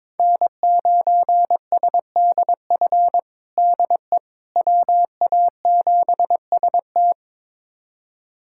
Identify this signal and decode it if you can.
Morse code, heard as a keyed tone, about 22 wpm, 700 Hz. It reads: N9SDF DE WA7ST
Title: USA Callsign DE Callsign - 5 Characters - 22wpm